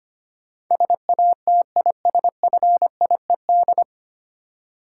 Morse code, heard as a keyed tone, about 25 wpm, 700 Hz. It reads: SATISFIED
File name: Nine Letter Words - 25wpm